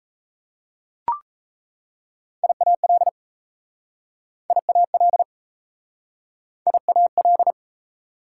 Morse code, heard as a keyed tone, about 38 wpm, 700 Hz. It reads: IAL IAL IAL